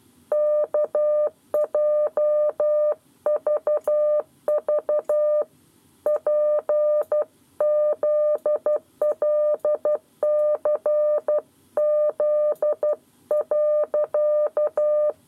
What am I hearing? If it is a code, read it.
Morse code, heard as a keyed tone, about 12 wpm, 600 Hz. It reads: KJVV PZLCZ.
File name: 7 Lezione cw 1